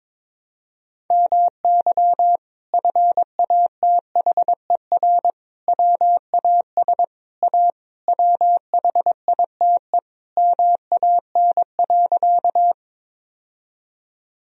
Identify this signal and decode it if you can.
Morse code, heard as a keyed tone, about 22 wpm, 700 Hz. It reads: MY FATHER WAS A WHITE MAN.
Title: Sentences from Top 1000 Words - Review Part 1 - 22wpm